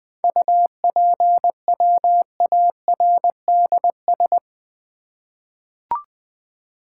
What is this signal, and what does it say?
Morse code, heard as a keyed tone, about 20 wpm, 700 Hz. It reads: UPWARDS
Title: 7 Letter Words - First 3 Letters Given - 20wpm